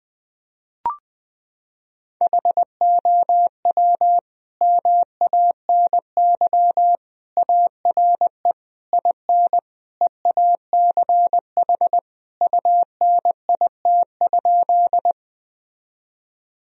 Morse code, heard as a keyed tone, about 21 wpm, 700 Hz. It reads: HOW MANY ARE IN EACH UNIT?
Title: Sentences from Top 1000 Words - Review Part 2 - 20wpm